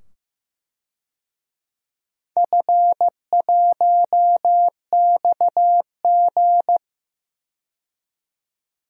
Morse code, heard as a keyed tone, about 15 wpm, 700 Hz. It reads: F1XG
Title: DXCC-100 Call Signs - 4 Characters - 15wpm